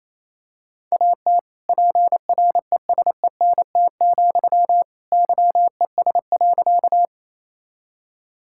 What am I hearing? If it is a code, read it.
Morse code, heard as a keyed tone, about 28 wpm, 700 Hz. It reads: AT PRESENT, YES.
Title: Sentences from Top 1000 Words - Review Part 5 - 28wpm